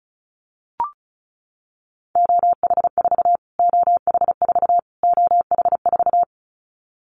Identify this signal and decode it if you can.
Morse code, heard as a keyed tone, about 34 wpm, 700 Hz. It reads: OH4 OH4 OH4
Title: Callsign Prefixes for Top 25 DXCC - Repeated 3x - 35wpm